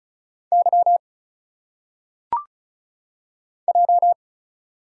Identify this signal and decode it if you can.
Morse code, heard as a keyed tone, about 34 wpm, 700 Hz. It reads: Y J
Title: Single Letters - Phonetic Pronunciation 35wpm